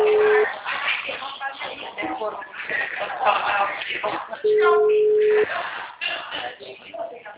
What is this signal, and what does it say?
Ringback tone (Europe, 425 Hz), with speech.